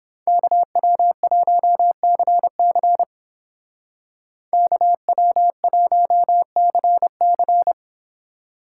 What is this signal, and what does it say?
Morse code, heard as a keyed tone, about 30 wpm, 700 Hz. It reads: KW1CC KW1CC